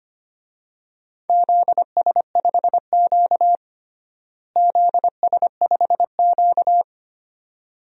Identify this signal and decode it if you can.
Morse code, heard as a keyed tone, about 25 wpm, 700 Hz. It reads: ZS5Q ZS5Q
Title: DXCC-100 Call Signs - 4 Characters - Repeated 2x - 25wpm